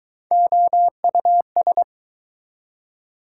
Morse code, heard as a keyed tone, about 23 wpm, 700 Hz. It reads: OUS